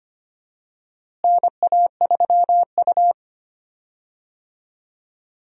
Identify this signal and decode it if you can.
Morse code, heard as a keyed tone, about 25 wpm, 700 Hz. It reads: NA3U
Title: USA Call Signs - 4 Characters - Repeated 3x - 25wpm